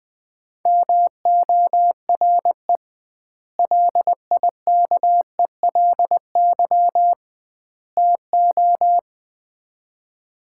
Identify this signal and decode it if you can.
Morse code, heard as a keyed tone, about 20 wpm, 700 Hz. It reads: MORE LIKELY TO
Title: 2x Word-Spacing - Sets of 3 Words - 20wpm